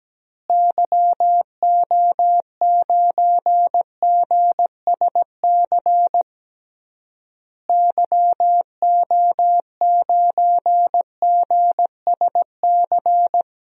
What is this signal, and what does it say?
Morse code, heard as a keyed tone, about 17 wpm, 700 Hz. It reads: YO9GSC YO9GSC
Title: DXCC-100 Call Signs - 6 Characters - Repeated 3x - 17wpm